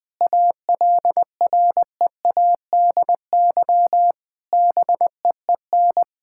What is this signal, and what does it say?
Morse code, heard as a keyed tone, about 20 wpm, 700 Hz. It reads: ALREADY BEEN
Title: Sentences from Top 1000 Words - Review Part 5 - 20wpm